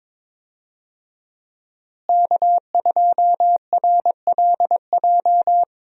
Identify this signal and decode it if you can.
Morse code, heard as a keyed tone, about 22 wpm, 700 Hz. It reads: K2RLJ
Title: USA Call Signs - 5 Characters - Repeated 2x - 22wpm